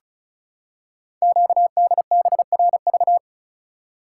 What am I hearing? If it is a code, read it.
Morse code, heard as a keyed tone, about 36 wpm, 700 Hz. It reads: QDBRV
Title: Code Groups - 5 Letters - 35wpm